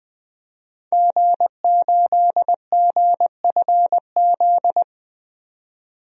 Morse code, heard as a keyed tone, about 20 wpm, 700 Hz. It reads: G8GFZ
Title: DXCC-100 Call Signs - 5 Characters - 20wpm